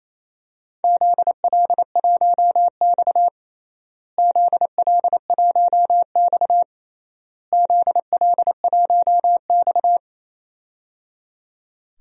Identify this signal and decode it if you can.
Morse code, heard as a keyed tone, about 27 wpm, 700 Hz. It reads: ZL1X ZL1X ZL1X
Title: DXCC-100 Call Signs - 4 Characters - Repeated 3x - 28wpm